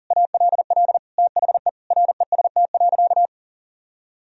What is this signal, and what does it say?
Morse code, heard as a keyed tone, about 40 wpm, 700 Hz. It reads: ALL THE REST.
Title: Sentences from Top 1000 Words - Review Part 1 - 40wpm